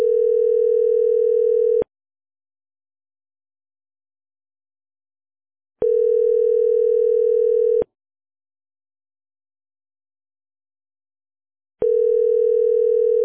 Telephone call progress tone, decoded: ringback tone